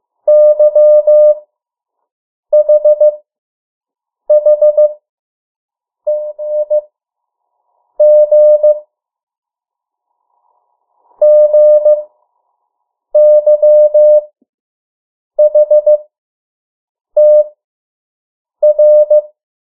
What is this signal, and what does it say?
Morse code, heard as a keyed tone, about 15 wpm, spaced as for 6 wpm, 600 Hz. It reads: YHHEIG GYHTR
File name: morse mp3 / 03 urok 2